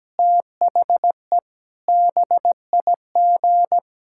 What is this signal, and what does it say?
Morse code, heard as a keyed tone, about 17 wpm, 700 Hz. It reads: THE BIG
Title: Pangram Challenge v2 - 17wpm